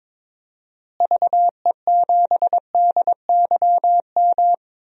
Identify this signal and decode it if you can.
Morse code, heard as a keyed tone, about 22 wpm, 700 Hz. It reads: VE7DYM
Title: DXCC-100 Call Signs - 6 Characters - Repeated 2x - 22wpm